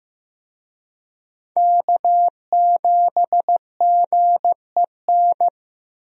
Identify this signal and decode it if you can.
Morse code, heard as a keyed tone, about 15 wpm, 700 Hz. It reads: K7GEN
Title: USA Call Signs - 5 Characters - Repeated 3x - 15wpm